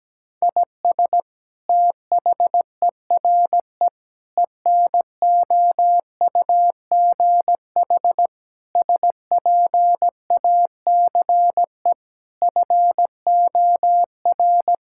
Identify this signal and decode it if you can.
Morse code, heard as a keyed tone, about 17 wpm, 700 Hz. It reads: IS THERE ENOUGH SPACE FOR
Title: Sentences from Top 1000 Words - Review Part 5 - 17wpm